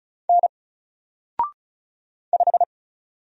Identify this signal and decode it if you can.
Morse code, heard as a keyed tone, about 33 wpm, 700 Hz. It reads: N 5
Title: Single Letter-Number 35wpm